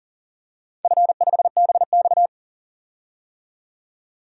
Morse code, heard as a keyed tone, about 39 wpm, 700 Hz. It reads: F5BX
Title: DXCC-100 Call Signs - 4 Characters - 40wpm